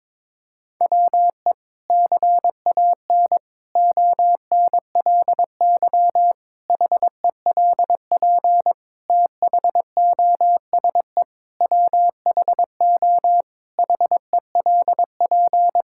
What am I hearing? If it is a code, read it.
Morse code, heard as a keyed tone, about 22 wpm, 700 Hz. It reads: WE CAN ONLY HELP THOSE WHO HELP
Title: Sentences from Top 1000 Words - Review Part 5 - 22wpm